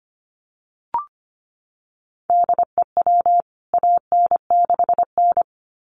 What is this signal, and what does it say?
Morse code, heard as a keyed tone, about 25 wpm, 700 Hz. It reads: DEW AN6N
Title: CWT Exchange 25wpm